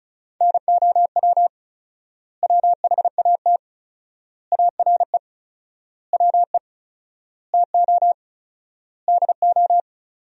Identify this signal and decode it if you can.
Morse code, heard as a keyed tone, about 34 wpm, 700 Hz. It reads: NOW WHAT ARE WE TO DO
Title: 4x Word-Spacing - Sentences from Top 100 Words - 35wpm